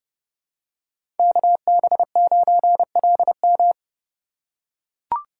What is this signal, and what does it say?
Morse code, heard as a keyed tone, about 32 wpm, 700 Hz. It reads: KB9LM
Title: USA Call Signs - 5 Characters - 3x Character Spacing - 30wpm